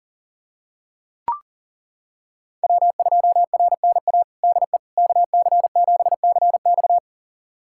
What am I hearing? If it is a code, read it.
Morse code, heard as a keyed tone, about 40 wpm, 700 Hz. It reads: W2RNA DE KC7CX
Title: USA Callsign DE Callsign - 5 Characters - 40wpm